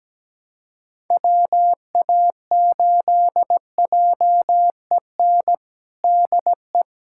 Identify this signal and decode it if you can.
Morse code, heard as a keyed tone, about 17 wpm, 700 Hz. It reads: WA8JEN DE
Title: USA Callsign DE Callsign 17wpm